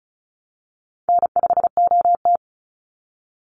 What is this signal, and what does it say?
Morse code, heard as a keyed tone, about 34 wpm, 700 Hz. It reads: N5OT
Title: Easy Call Signs Repeated 2x - 35wpm